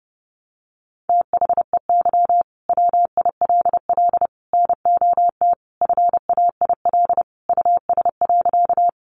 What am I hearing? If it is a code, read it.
Morse code, heard as a keyed tone, about 30 wpm, 700 Hz. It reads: THEY WILL NOT FAIL US.
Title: Sentences from Top 1100 Words 30wpm